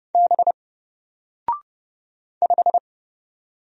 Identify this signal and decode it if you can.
Morse code, heard as a keyed tone, about 30 wpm, 700 Hz. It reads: B 5
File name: Single-Letter-Number-30wpm